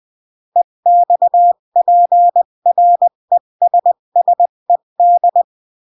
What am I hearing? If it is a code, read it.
Morse code, heard as a keyed tone, about 20 wpm, 700 Hz. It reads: EXPRESSED
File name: CW / jodorowsky / 82 commandments 20-15